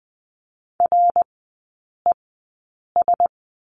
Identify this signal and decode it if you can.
Morse code, heard as a keyed tone, about 20 wpm, spaced as for 8 wpm, 700 Hz. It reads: RES
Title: 053 Lesson 10e - Letter r - Three Characters 20wpm